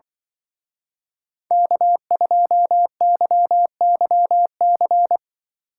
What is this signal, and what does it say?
Morse code, heard as a keyed tone, about 25 wpm, 700 Hz. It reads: K2YYC